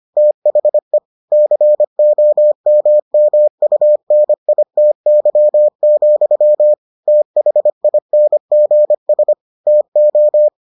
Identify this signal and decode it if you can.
Morse code, heard as a keyed tone, about 25 wpm, 600 Hz. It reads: THE COMMUNITY, THINGS TO